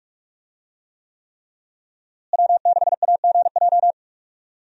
Morse code, heard as a keyed tone, about 46 wpm, 700 Hz. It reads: W6AGJ
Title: USA Call Signs - 5 Characters - Repeated 3x - 45wpm